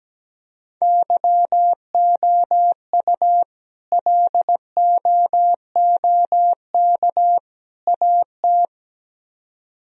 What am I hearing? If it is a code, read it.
Morse code, heard as a keyed tone, about 17 wpm, 700 Hz. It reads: YOU LOOK AT
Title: Speed-Racing - Sets of 3 Words 17wpm